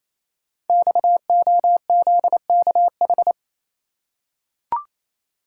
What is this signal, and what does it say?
Morse code, heard as a keyed tone, about 28 wpm, 700 Hz. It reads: XOZKH
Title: Code Groups - 5 Letter:Numbers - 28wpm